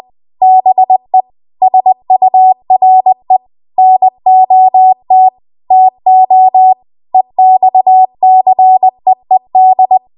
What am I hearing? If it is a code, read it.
Morse code, heard as a keyed tone, about 20 wpm, 750 Hz. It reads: BE SURE NOT TO EXCEED
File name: Archive / 20 WPM / 20WPM